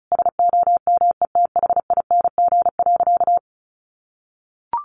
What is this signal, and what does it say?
Morse code, heard as a keyed tone, about 34 wpm, 700 Hz. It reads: SOMETHING.
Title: Sentences from Top 1000 Words 35wpm